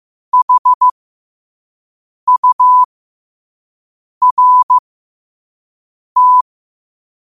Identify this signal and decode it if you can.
Morse code, heard as a keyed tone, about 15 wpm, spaced as for 5 wpm, 1000 Hz. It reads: HURT
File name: qotdfiles / 5wpm